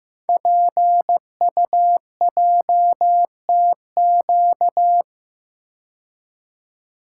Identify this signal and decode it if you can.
Morse code, heard as a keyed tone, about 15 wpm, 700 Hz. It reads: PUJTQ
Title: Code Groups - 5 Letter:Numbers - 15wpm